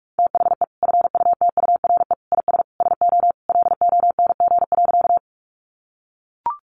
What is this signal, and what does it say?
Morse code, heard as a keyed tone, about 44 wpm, 700 Hz. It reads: THE FUTURE IS SO LONG.
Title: Sentences from Top 700 Words 45wpm